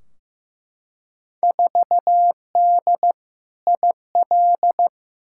Morse code, heard as a keyed tone, about 15 wpm, 700 Hz. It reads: 4D IL
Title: Field Day Exchanges Encore - 15wpm